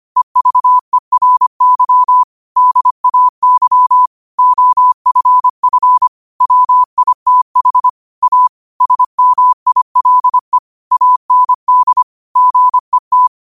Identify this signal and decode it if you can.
Morse code, heard as a keyed tone, about 25 wpm, 1000 Hz. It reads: EVERY DAY OFF WITH A SMILE AND GET
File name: qotdfiles / 25wpm